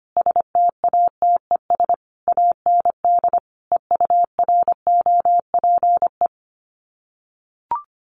Morse code, heard as a keyed tone, about 25 wpm, 700 Hz. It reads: STATES AND EUROPE
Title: Sets of 5 Words Encore 25wpm